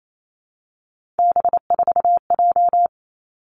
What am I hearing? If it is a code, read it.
Morse code, heard as a keyed tone, about 28 wpm, 700 Hz. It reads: B4J
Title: Three Challenging Letter-Number Combinations 28wpm